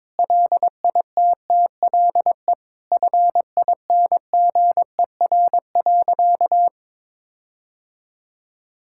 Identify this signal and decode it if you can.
Morse code, heard as a keyed tone, about 22 wpm, 700 Hz. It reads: LITTLE FINGER.
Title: Sentences from Top 1600 Words 22wpm